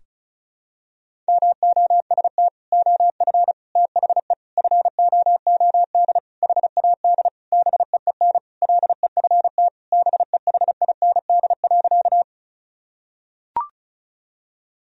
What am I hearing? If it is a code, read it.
Morse code, heard as a keyed tone, about 36 wpm, 700 Hz. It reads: MOST OF THE FOOD HAD BEEN LEFT BEHIND.
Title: Sentences from Top 1000 Words - Review Part 1 - 35wpm